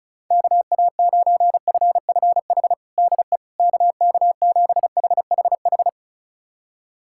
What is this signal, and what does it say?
Morse code, heard as a keyed tone, about 34 wpm, 700 Hz. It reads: KA9FFH DE KK7HHH
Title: USA Callsign DE Callsign - 6 Characters - 35wpm